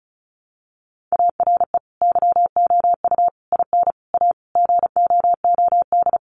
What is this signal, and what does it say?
Morse code, heard as a keyed tone, about 34 wpm, 700 Hz. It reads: ARE YOU IN A GOOD
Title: Sentences from Top 1100 Words 35wpm